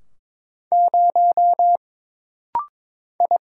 Morse code, heard as a keyed tone, about 22 wpm, 700 Hz. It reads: 0 I
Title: Single Letter-Number - Rapid-Fire 22wpm